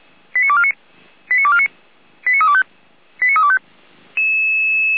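Five-tone selective call (ZVEI), heard as quick runs of five tones, tones of about 70 ms. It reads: 89249 89249 89325 89325
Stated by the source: ZVEI five-tone selective call received with an SDR, archived by Feuerwehr Orlamünde.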